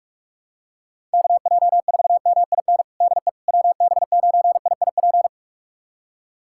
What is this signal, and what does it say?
Morse code, heard as a keyed tone, about 42 wpm, 700 Hz. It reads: KJ4MIN DE WB9IIP
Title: USA Callsign DE Callsign - 6 Characters - 45wpm